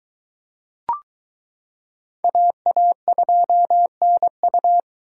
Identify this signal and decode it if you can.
Morse code, heard as a keyed tone, about 22 wpm, 700 Hz. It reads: AA2NU